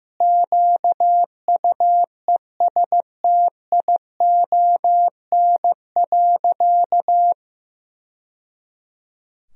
Morse code, heard as a keyed tone, about 15 wpm, 700 Hz. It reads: QUESTION.
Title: Sentences from Top 1000 Words - Review Part 3 - 15wpm